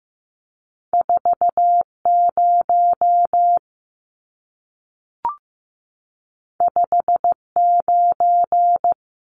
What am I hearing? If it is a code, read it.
Morse code, heard as a keyed tone, about 15 wpm, 700 Hz. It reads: 40 59
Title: Numbers - 2-Digit 15wpm